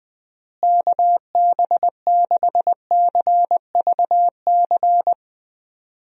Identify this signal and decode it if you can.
Morse code, heard as a keyed tone, about 20 wpm, 700 Hz. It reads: KB6CVC
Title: USA Call Signs - 6 Characters - Repeated 2x - 20wpm